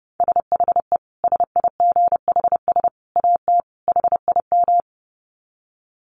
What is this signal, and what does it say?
Morse code, heard as a keyed tone, about 30 wpm, 700 Hz. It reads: SHE SIGHS AT HIM
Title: Dit Challenge 30wpm